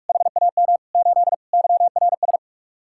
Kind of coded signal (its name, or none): Morse code